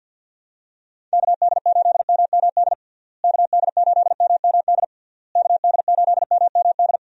Morse code, heard as a keyed tone, about 48 wpm, 700 Hz. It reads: KD8MMD KD8MMD KD8MMD
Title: USA Call Signs - 6 Characters - Repeated 3x - 50wpm